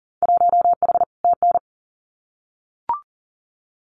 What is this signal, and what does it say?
Morse code, heard as a keyed tone, about 38 wpm, 700 Hz. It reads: 1H TN
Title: Winter Field Day Exchanges 40wpm